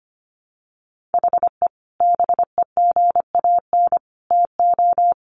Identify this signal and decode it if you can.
Morse code, heard as a keyed tone, about 25 wpm, 700 Hz. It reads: HE BEGAN TO
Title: Sentences from Top 500 Words 25wpm